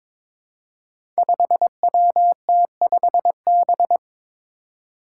Morse code, heard as a keyed tone, about 22 wpm, 700 Hz. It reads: 5WT5B
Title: Five Challenging Letter-Number Combinations 22wpm